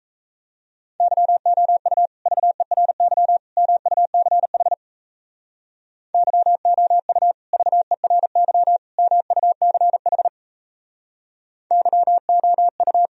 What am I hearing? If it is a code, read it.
Morse code, heard as a keyed tone, about 33 wpm, 700 Hz. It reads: YOU VERY MUCH YOU VERY MUCH YOU
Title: Speed-Racing - Sets of 3 Words 28wpm